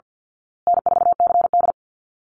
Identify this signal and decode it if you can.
Morse code, heard as a keyed tone, about 48 wpm, 700 Hz. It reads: N4CD
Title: Easy Call Signs Repeated 2x - 50wpm